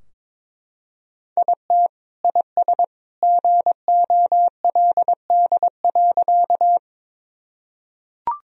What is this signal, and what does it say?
Morse code, heard as a keyed tone, about 22 wpm, 700 Hz. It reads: IT IS GOLD.
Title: Sentences from Top 1000 Words - Review Part 3 - 22wpm